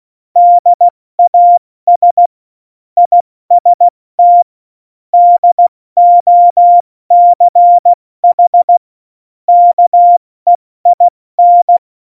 Morse code, heard as a keyed tone, about 16 wpm, 700 Hz. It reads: DAS IST DOCH KEIN
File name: Morsekurs / Witze / Witz 13. Februar 2018, 70 F80